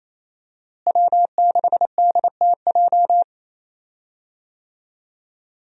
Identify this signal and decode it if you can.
Morse code, heard as a keyed tone, about 28 wpm, 700 Hz. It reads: W6DTJ
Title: USA Call Signs - 5 Characters - Repeated 2x - 28wpm